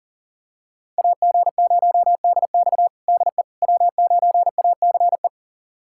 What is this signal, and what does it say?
Morse code, heard as a keyed tone, about 40 wpm, 700 Hz. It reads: AG0DX DE W9ACE
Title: USA Callsign DE Callsign - 5 Characters - 40wpm